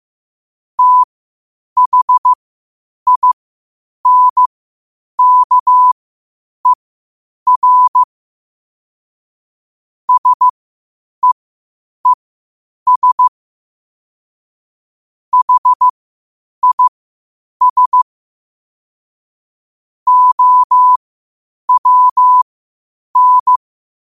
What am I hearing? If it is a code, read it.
Morse code, heard as a keyed tone, about 15 wpm, spaced as for 8 wpm, 1000 Hz. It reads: THINKER SEES HIS OWN
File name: 8wpm